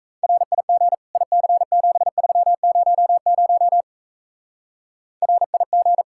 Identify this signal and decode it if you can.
Morse code, heard as a keyed tone, about 36 wpm, 700 Hz. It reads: RIG IC7300 RIG